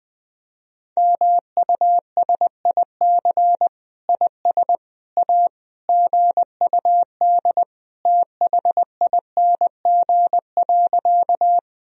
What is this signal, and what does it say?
Morse code, heard as a keyed tone, about 20 wpm, 700 Hz. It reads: MUSIC IS A GUD THING.
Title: Sentences from Top 300 Words and Abbreviations 20wpm